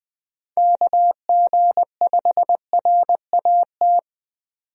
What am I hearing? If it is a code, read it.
Morse code, heard as a keyed tone, about 21 wpm, 700 Hz. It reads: KG5RAT
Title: USA Call Signs - 6 Characters - 20wpm